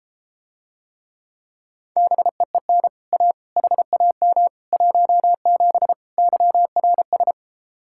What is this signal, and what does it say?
Morse code, heard as a keyed tone, about 33 wpm, 700 Hz. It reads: BEEN A HAM 17 YRS